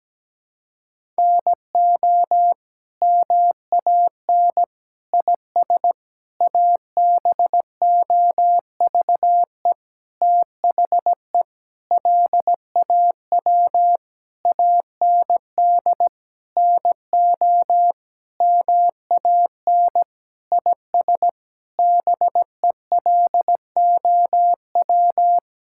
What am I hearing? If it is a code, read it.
Morse code, heard as a keyed tone, about 17 wpm, 700 Hz. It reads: NO MAN IS ABOVE THE LAW AND NO MAN IS BELOW
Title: Sentences from Top 1000 Words - Review Part 5 - 17wpm